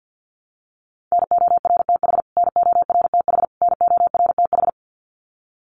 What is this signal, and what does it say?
Morse code, heard as a keyed tone, about 50 wpm, 700 Hz. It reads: NORTH NORTH NORTH
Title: Top 500 Words Repeated 50wpm